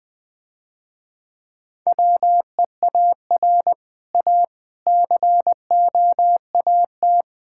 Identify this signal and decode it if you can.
Morse code, heard as a keyed tone, about 20 wpm, 700 Hz. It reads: WEAR A COAT
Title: Difficult Sentences with Hints - 20wpm